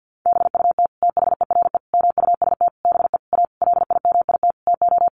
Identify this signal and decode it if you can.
Morse code, heard as a keyed tone, about 50 wpm, 700 Hz. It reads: BUT THERE MUST BE A LIMIT TO
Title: Sentences from Top 1500 Words 50wpm